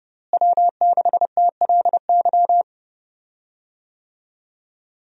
Morse code, heard as a keyed tone, about 30 wpm, 700 Hz. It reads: W6TLY
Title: USA Call Signs - 5 Characters - Repeated 3x - 30wpm